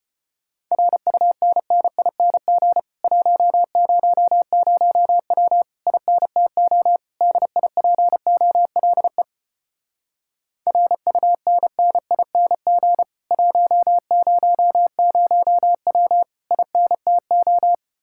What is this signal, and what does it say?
Morse code, heard as a keyed tone, about 33 wpm, 700 Hz. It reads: RUNNING 100W INTO DIPOLE RUNNING 100W INTO